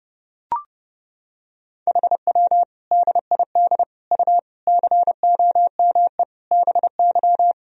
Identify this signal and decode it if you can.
Morse code, heard as a keyed tone, about 30 wpm, 700 Hz. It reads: HW DID U COME BY
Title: Sentences from Top 100 Words and Abbreviations 30wpm